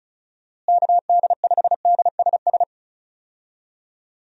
Morse code, heard as a keyed tone, about 34 wpm, 700 Hz. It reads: KD5DSS
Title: USA Call Signs - 6 Characters - Repeated 2x - 35wpm